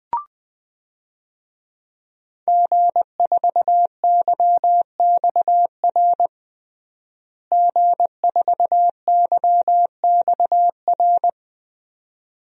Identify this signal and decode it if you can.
Morse code, heard as a keyed tone, about 20 wpm, 700 Hz. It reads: G4YXR G4YXR
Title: DXCC-100 Call Signs - 5 Characters - Repeated 2x - 20wpm